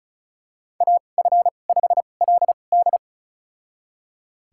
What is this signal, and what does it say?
Morse code, heard as a keyed tone, about 36 wpm, spaced as for 25 wpm, 700 Hz. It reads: AF5LD
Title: USA Call Signs - 2x Character Spacing - 35wpm